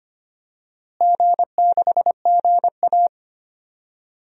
Morse code, heard as a keyed tone, about 25 wpm, 700 Hz. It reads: G6GA
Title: DXCC-100 Call Signs - 4 Characters - Repeated 3x - 25wpm